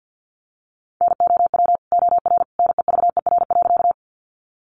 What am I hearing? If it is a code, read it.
Morse code, heard as a keyed tone, about 50 wpm, 700 Hz. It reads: NOW OR NEVER.
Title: Sentences from Top 300 Words 50wpm